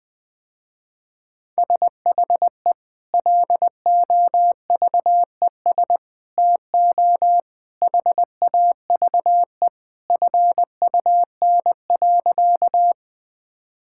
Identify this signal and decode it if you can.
Morse code, heard as a keyed tone, about 21 wpm, 700 Hz. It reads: SHE LOVES TO HAVE FUN.
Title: Sentences with 2-Word Phrases - Spoken Beginning - 20wpm